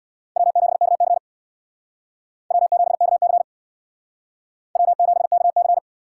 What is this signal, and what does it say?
Morse code, heard as a keyed tone, about 60 wpm, 700 Hz. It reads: W7KC W7KC W7KC
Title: Speed Racing - USA 1x2 Call Signs 50wpm